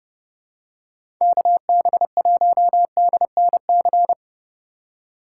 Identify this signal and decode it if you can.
Morse code, heard as a keyed tone, about 30 wpm, 700 Hz. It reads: KB1DNC
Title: USA Call Signs - 6 Characters - 4x Character Spacing - 30wpm